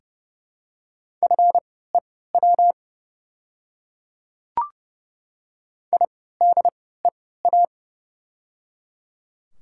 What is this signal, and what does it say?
Morse code, heard as a keyed tone, about 30 wpm, spaced as for 17 wpm, 700 Hz. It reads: FEW IDEA
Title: 3x Character Spacing - Top 500 Words - 30wpm